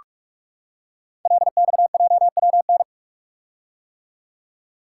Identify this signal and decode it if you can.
Morse code, heard as a keyed tone, about 46 wpm, 700 Hz. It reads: LXJWN
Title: Code Groups - 5 Letter:Numbers - 45wpm